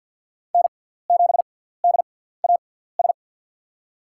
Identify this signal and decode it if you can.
Morse code, heard as a keyed tone, about 49 wpm, spaced as for 17 wpm, 700 Hz. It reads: N7DAS
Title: USA Call Signs - 6x Character Spacing - 50wpm